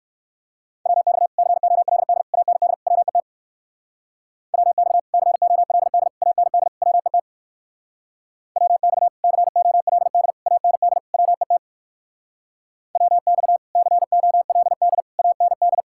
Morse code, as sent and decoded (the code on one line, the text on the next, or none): .-- -..- / -.-. --- .-.. -.. / .- -. -.. / .-- . - / .-- -..- / -.-. --- .-.. -.. / .- -. -.. / .-- . - / .-- -..- / -.-. --- .-.. -.. / .- -. -.. / .-- . - / .-- -..- / -.-. --- .-.. -.. / .- -. -..
WX COLD AND WET WX COLD AND WET WX COLD AND WET WX COLD AND